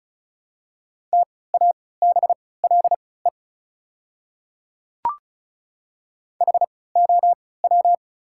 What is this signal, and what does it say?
Morse code, heard as a keyed tone, about 36 wpm, spaced as for 20 wpm, 690 Hz. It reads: TABLE HOW
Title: 3x Character Spacing - Top 500 Words - 35wpm